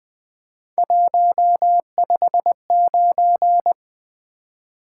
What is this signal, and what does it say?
Morse code, heard as a keyed tone, about 20 wpm, 700 Hz. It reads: 159
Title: Three Letter-Number Combinations - 20wpm